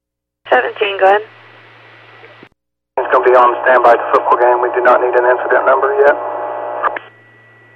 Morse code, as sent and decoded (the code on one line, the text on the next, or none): none